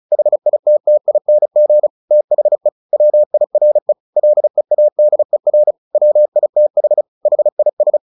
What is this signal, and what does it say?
Morse code, heard as a keyed tone, about 34 wpm, 600 Hz. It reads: HITTING THE WIRE LEADER WITH HIS
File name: the old man and the sea 2 35wpm